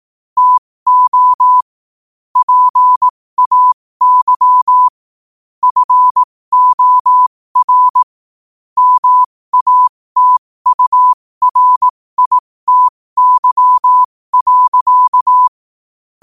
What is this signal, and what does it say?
Morse code, heard as a keyed tone, about 18 wpm, spaced as for 15 wpm, 1000 Hz. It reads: TO PAY FOR MATURITY.
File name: qotdfiles / 15wpm